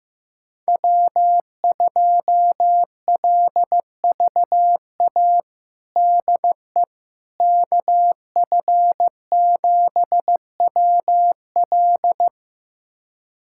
Morse code, heard as a keyed tone, about 15 wpm, 700 Hz. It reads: W2LVA DE KF7WL
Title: USA Callsign DE Callsign - 5 Characters - 15wpm